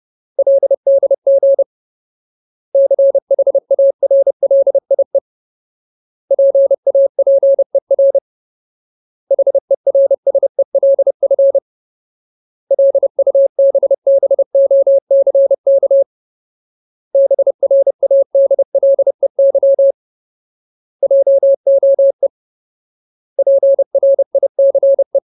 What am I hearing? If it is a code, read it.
Morse code, heard as a keyed tone, about 30 wpm, 550 Hz. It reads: LDG CHARLIE PAPER HERSELF LUBBOCK BRADLEY JOE PRICE
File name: Ultimate Word List / 30WPM Ultimate